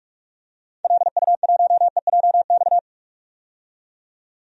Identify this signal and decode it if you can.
Morse code, heard as a keyed tone, about 42 wpm, 690 Hz. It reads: LU1EJX
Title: DXCC-100 Call Signs - 6 Characters - 45wpm